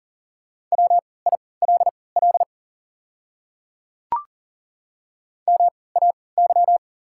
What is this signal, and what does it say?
Morse code, heard as a keyed tone, about 40 wpm, spaced as for 23 wpm, 690 Hz. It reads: WILL MAY
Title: 3x Character Spacing - Top 100 Words - 40wpm